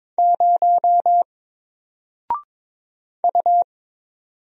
Morse code, heard as a keyed tone, about 22 wpm, 700 Hz. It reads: 0 U
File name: Single-Letter-Number-22wpm